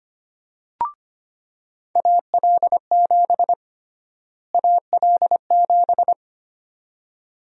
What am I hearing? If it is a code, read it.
Morse code, heard as a keyed tone, about 25 wpm, 700 Hz. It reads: AL7 AL7
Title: USA Call Sign Prefixes - Repeated 2x - 25wpm